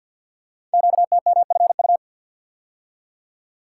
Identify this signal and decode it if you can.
Morse code, heard as a keyed tone, about 50 wpm, 690 Hz. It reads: QTMFU
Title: Code Groups - 5 Letters - 50wpm